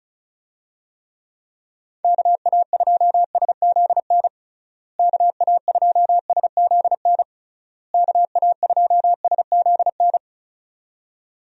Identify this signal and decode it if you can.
Morse code, heard as a keyed tone, about 36 wpm, 700 Hz. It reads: KA2SZN KA2SZN KA2SZN
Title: USA Call Signs - 6 Characters - Repeated 3x - 35wpm